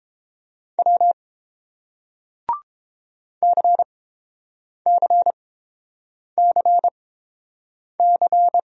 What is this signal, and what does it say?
Morse code, heard as a keyed tone, about 30 wpm, spaced as for 7 wpm, 700 Hz. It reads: W CCCC